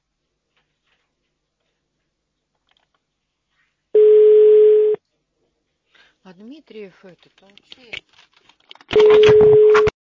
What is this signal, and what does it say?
Ringback tone (Europe, 425 Hz), with speech.